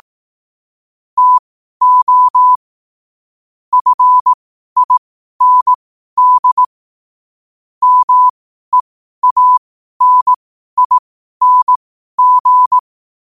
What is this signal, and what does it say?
Morse code, heard as a keyed tone, about 18 wpm, spaced as for 12 wpm, 1000 Hz. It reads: TO FIND MEANING
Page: New quotes every day in morse code at 12 Words per minute.